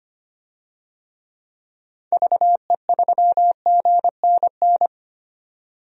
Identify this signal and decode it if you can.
Morse code, heard as a keyed tone, about 25 wpm, 690 Hz. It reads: VE3GNN
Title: DXCC-100 Call Signs - 6 Characters - 25wpm